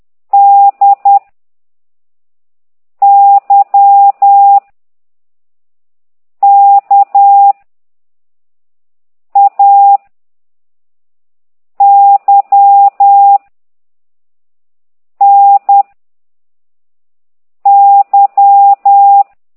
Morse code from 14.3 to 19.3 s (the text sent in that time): N Y